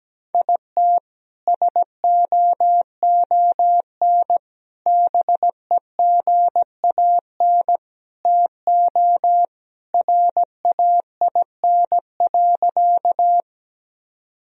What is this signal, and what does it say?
Morse code, heard as a keyed tone, about 17 wpm, 700 Hz. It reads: IT SOON BEGAN TO RAIN.
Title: Sentences from Top 1000 Words - Review Part 1 - 17wpm